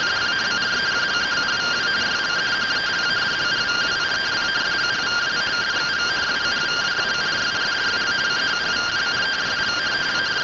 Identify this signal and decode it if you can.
Two-tone teleprinter signal, about 170 Hz shift.